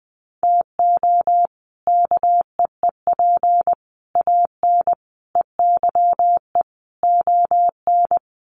Morse code, heard as a keyed tone, about 20 wpm, 700 Hz. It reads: TO KEEP AN EYE ON
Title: Sets of 5 Words Encore 20wpm